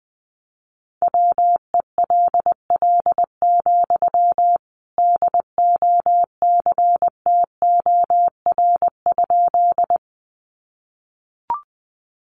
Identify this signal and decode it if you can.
Morse code, heard as a keyed tone, about 20 wpm, 700 Hz. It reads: WELL, DOCTOR?
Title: Sentences from Top 1500 Words 20wpm